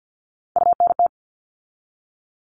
Morse code, heard as a keyed tone, about 50 wpm, 700 Hz. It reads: UNT
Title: All Cut Numbers - 3-Digit 50wpm